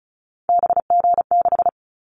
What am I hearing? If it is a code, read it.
Morse code, heard as a keyed tone, about 34 wpm, 700 Hz. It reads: BG6
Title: Callsign Prefixes for Top 100 DXCC - 35wpm